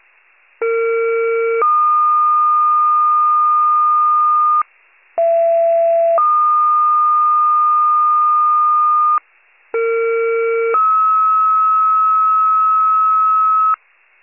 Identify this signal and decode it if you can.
Two-tone page, heard as a one-second tone, then a three-second tone, about 460 then 1220 Hz, about 670 then 1220 Hz, about 460 then 1290 Hz. It is heard from 0.6 to 13.7 s.
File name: pager / 41 RESCUE DUTY / wcalert